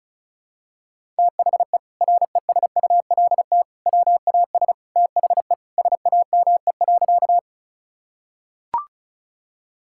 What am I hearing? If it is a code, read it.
Morse code, heard as a keyed tone, about 34 wpm, 700 Hz. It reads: THE RESULT WAS THE SAME.
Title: Sentences from Top 1000 Words - Review Part 5 - 35wpm